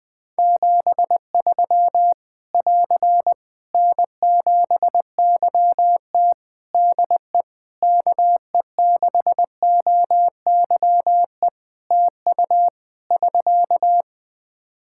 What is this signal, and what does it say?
Morse code, heard as a keyed tone, about 21 wpm, 700 Hz. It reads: 73 <AR> N7YT DE KE6OYE TU <SK>
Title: Practice QSO - Exchange 4 - Ending - 20wpm